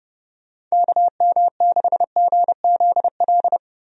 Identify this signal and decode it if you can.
Morse code, heard as a keyed tone, about 30 wpm, 700 Hz. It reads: KM6GZL
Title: USA Call Signs - 6 Characters - 30wpm